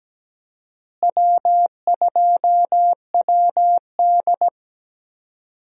Morse code, heard as a keyed tone, about 17 wpm, 700 Hz. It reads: W2WD
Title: USA Call Signs - 4 Characters - Repeated 2x - 17wpm